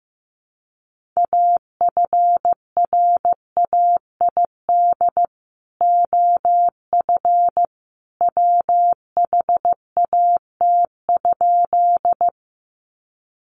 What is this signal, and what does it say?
Morse code, heard as a keyed tone, about 15 wpm, 700 Hz. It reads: AFRAID OF WHAT?
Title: Sentences from Top 1500 Words 15wpm